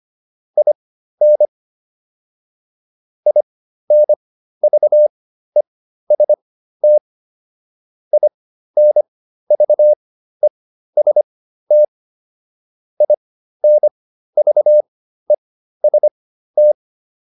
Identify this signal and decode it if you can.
Morse code, heard as a keyed tone, about 25 wpm, spaced as for 13 wpm, 600 Hz. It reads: IN INVEST INVEST INVEST